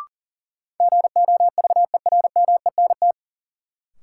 Morse code, heard as a keyed tone, about 40 wpm, 700 Hz. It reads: GOVERMENT
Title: Sentences from Top 1300 Words - New Word Drill Alt - 40wpm